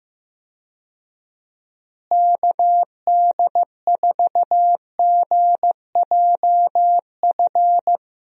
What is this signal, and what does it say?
Morse code, heard as a keyed tone, about 15 wpm, 700 Hz. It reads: KD4GJF
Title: USA Call Signs - 6 Characters - Repeated 3x - 15wpm